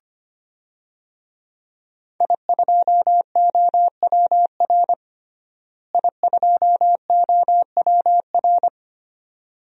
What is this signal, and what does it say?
Morse code, heard as a keyed tone, about 25 wpm, 700 Hz. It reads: I2OWR I2OWR
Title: DXCC-100 Call Signs - 5 Characters - Repeated 2x - 25wpm